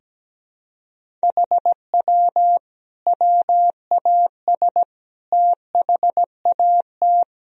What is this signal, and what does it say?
Morse code, heard as a keyed tone, about 17 wpm, 700 Hz. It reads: HW WAS THAT
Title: Sentences from Top 500 Words and Abbreviations 17wpm